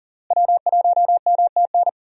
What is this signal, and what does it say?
Morse code, heard as a keyed tone, about 40 wpm, 700 Hz. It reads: W1MTN
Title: USA Call Signs - 5 Characters - Repeated 3x - 40wpm